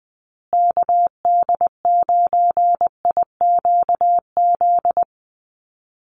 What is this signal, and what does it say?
Morse code, heard as a keyed tone, about 20 wpm, 700 Hz. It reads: KD9IQZ
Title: USA Call Signs Repeated 2x 20wpm